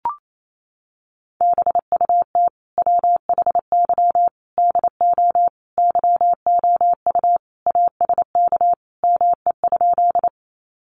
Morse code, heard as a keyed tone, about 28 wpm, 700 Hz. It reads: BUT WHY DO YOU ASK ME?
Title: Sentences from Top 200 Words 28wpm